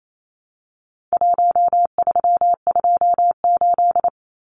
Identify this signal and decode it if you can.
Morse code, heard as a keyed tone, about 28 wpm, 700 Hz. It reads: 1328
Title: Numbers - 4-Digit 28wpm